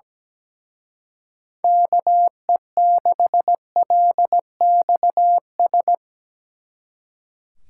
Morse code, heard as a keyed tone, about 20 wpm, 700 Hz. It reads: KE6LXS
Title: Speed Racing - USA 2x3 Call Signs 17wpm